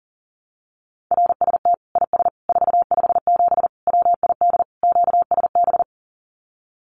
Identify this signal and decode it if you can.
Morse code, heard as a keyed tone, about 40 wpm, 700 Hz. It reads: RST IS 457 WID QSB
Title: Common QSO Phrases 40wpm